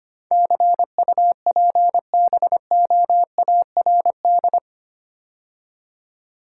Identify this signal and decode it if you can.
Morse code, heard as a keyed tone, about 26 wpm, 700 Hz. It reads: CUPBOARD
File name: 8 Letter Words - First 3 Letters Given - 25wpm